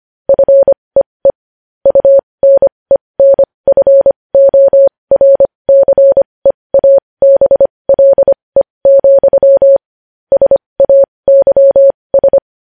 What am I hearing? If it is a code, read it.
Morse code, heard as a keyed tone, about 25 wpm, spaced as for 20 wpm, 550 Hz. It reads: FEE UNENFORCEABLE, SAYS
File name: podcast / News headlines-1x-25x20